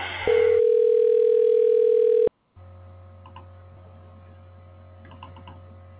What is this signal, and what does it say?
Ringback tone (North America), 440 + 480 Hz.